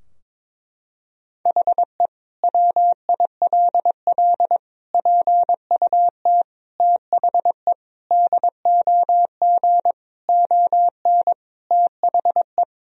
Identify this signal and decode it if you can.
Morse code, heard as a keyed tone, about 22 wpm, 700 Hz. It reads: HE WILL PUT THE DOG ON THE
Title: Sentences from Top 1000 Words - Review Part 2 - 22wpm